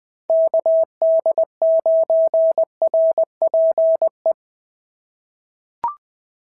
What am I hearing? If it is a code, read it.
Morse code, heard as a keyed tone, about 20 wpm, 690 Hz. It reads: KD9RPE
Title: POTA QSO Practice - 20wpm